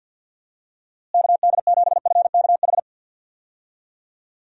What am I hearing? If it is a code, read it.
Morse code, heard as a keyed tone, about 51 wpm, 700 Hz. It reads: KD7FKH
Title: USA Call Signs - 6 Characters - Repeated 2x - 50wpm